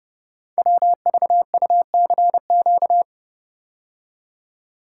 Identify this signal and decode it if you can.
Morse code, heard as a keyed tone, about 30 wpm, 700 Hz. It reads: WVUCQ
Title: Code Groups - 5 Letters - 30wpm